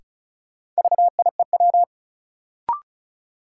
Morse code, heard as a keyed tone, about 34 wpm, 700 Hz. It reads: VIEW
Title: Sentences from Top 900 Words - New Word Drill - 35wpm